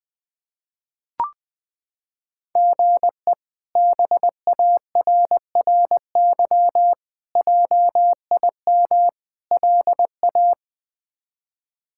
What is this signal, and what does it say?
Morse code, heard as a keyed tone, about 20 wpm, 700 Hz. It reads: GE BARRY JIM LA
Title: K1USN SST QSO Practice - 20wpm